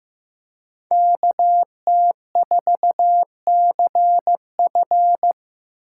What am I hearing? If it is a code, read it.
Morse code, heard as a keyed tone, about 18 wpm, 700 Hz. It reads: KT4CF
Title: Speed Racing - USA Call Signs 15wpm